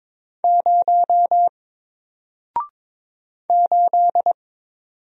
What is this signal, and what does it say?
Morse code, heard as a keyed tone, about 22 wpm, 700 Hz. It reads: 0 8
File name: Single-Letter-Number-22wpm